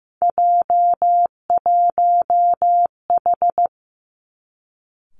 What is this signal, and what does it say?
Morse code, heard as a keyed tone, about 15 wpm, 700 Hz. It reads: J1H
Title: Three Challenging Letter-Number Combinations 15wpm